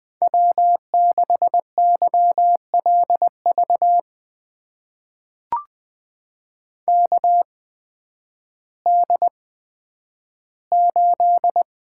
Morse code, heard as a keyed tone, about 20 wpm, spaced as for 11 wpm, 700 Hz. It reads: W6YLV KD8
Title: USA Call Signs - 8x Character Spacing - 20wpm